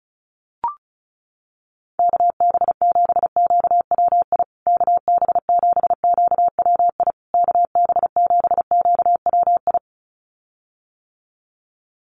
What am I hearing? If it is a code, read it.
Morse code, heard as a keyed tone, about 34 wpm, 700 Hz. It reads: KB7QWI KB7QWI KB7QWI
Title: USA Call Signs Repeated 3x 35wpm